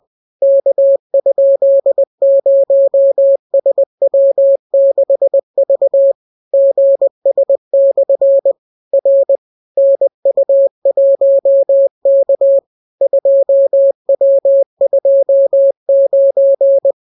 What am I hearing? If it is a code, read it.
Morse code, heard as a keyed tone, about 20 wpm, 550 Hz. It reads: K?0SW6V GS/ R NU1K 2W29